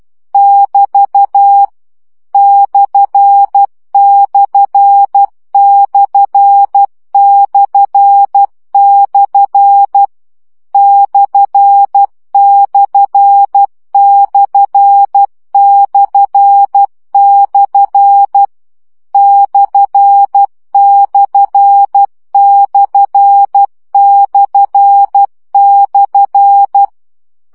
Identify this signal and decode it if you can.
Morse code, heard as a keyed tone, about 12 wpm, 800 Hz. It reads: <BT> ///// ///// /////
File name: CW gyakorlas per 10 baud